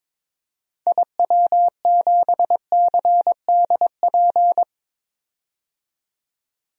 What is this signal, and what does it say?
Morse code, heard as a keyed tone, about 22 wpm, 700 Hz. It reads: IW7CDP
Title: DXCC-100 Call Signs - 6 Characters - 22wpm